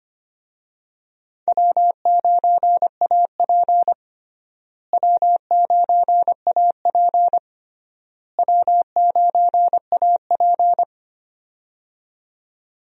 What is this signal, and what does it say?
Morse code, heard as a keyed tone, about 25 wpm, 700 Hz. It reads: W9AP W9AP W9AP
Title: USA Call Signs - 4 Characters - Repeated 3x - 25wpm